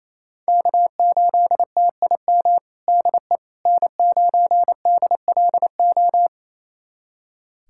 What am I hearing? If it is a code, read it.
Morse code, heard as a keyed tone, about 28 wpm, 700 Hz. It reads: K8TIM DE N9DLO
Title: USA Callsign DE Callsign - 5 Characters - 28wpm